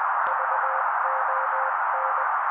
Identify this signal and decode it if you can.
Morse code, heard as a keyed tone, about 20 wpm, 530 Hz. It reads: VON
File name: daily / 18wpm noise